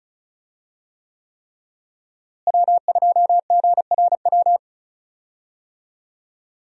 Morse code, heard as a keyed tone, about 34 wpm, 700 Hz. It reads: W2GRW
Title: USA Call Signs - 5 Characters - Repeated 3x - 35wpm